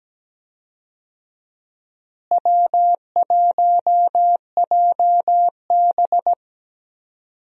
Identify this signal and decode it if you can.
Morse code, heard as a keyed tone, about 17 wpm, 700 Hz. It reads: W1JB
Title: USA Call Signs - 4 Characters - Repeated 2x - 17wpm